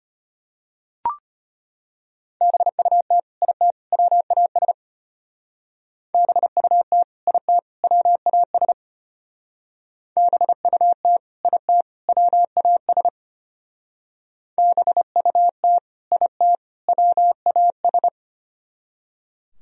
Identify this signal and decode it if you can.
Morse code, heard as a keyed tone, about 32 wpm, 700 Hz. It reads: BUT IT WAS BUT IT WAS BUT IT WAS BUT IT WAS